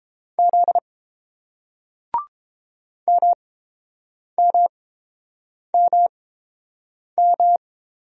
Morse code, heard as a keyed tone, about 27 wpm, spaced as for 7 wpm, 700 Hz. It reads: Z MMMM
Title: Speed-Racing Single Letters 22wpm